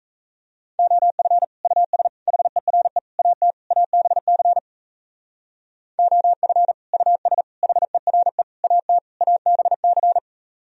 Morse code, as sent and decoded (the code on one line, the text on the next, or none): --- ..-. / ..- ... / .... .-. / .- - / .- -... -.-. / --- ..-. / ..- ... / .... .-. / .- - / .- -... -.-.
OF US HERE AT ABC OF US HERE AT ABC